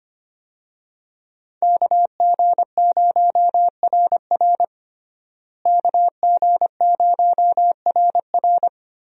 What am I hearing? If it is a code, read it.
Morse code, heard as a keyed tone, about 25 wpm, 700 Hz. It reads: KG0RR KG0RR
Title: USA Call Signs - 5 Characters - Repeated 3x - 25wpm